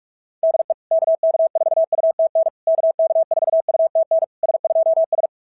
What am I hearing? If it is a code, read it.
Morse code, heard as a keyed tone, about 46 wpm, 690 Hz. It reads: DE KK4UTN KK4UTN S2S
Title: SOTA QSO Practice 45wpm